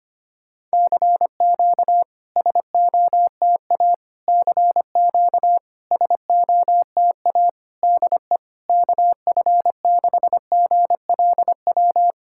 Morse code, heard as a keyed tone, about 25 wpm, 700 Hz. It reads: CQ SOTA CQ SOTA DE KF6GLW
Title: SOTA QSO Practice 25wpm